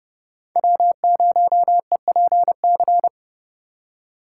Morse code, heard as a keyed tone, about 30 wpm, 700 Hz. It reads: W0EPC
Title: K1USN SST QSO Practice - 30wpm